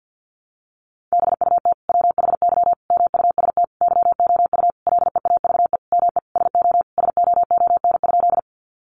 Morse code, heard as a keyed tone, about 50 wpm, 700 Hz. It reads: BUT WHY MUST YOU LEAVE ME SO SOON?
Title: Sentences from Top 400 Words 50wpm